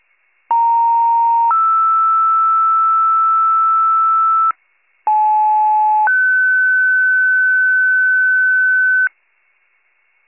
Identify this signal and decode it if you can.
Two-tone page, heard as a one-second tone, then a three-second tone, about 930 then 1320 Hz, about 860 then 1500 Hz. It is heard from 0.5 to 9.1 s.